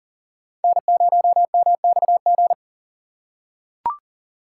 Morse code, heard as a keyed tone, about 40 wpm, 700 Hz. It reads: N0MXG
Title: USA Call Signs - 3x Character Spacing - 40wpm